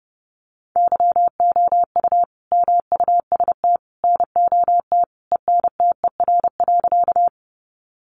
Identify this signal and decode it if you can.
Morse code, heard as a keyed tone, about 30 wpm, 700 Hz. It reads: YOU MUST NOT ENTER.
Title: Sentences from Top 1300 Words 30wpm